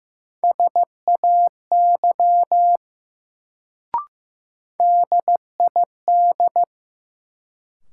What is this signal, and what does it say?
Morse code, heard as a keyed tone, about 15 wpm, 700 Hz. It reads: SAY DID
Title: Sets of 2 Words - New Word Drill - 15wpm